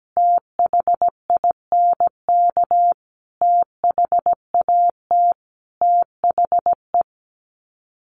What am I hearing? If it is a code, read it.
Morse code, heard as a keyed tone, about 17 wpm, 700 Hz. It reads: THINK THAT THE
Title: Sets of 5 Words 17wpm